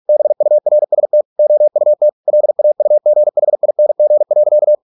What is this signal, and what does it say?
Morse code, heard as a keyed tone, about 46 wpm, 600 Hz. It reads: BURST OUT LAUGHING.